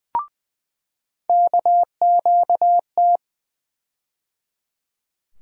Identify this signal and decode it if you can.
Morse code, heard as a keyed tone, about 20 wpm, 700 Hz. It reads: KQT
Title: USA Call Sign Postfixes - 3 Letters 20wpm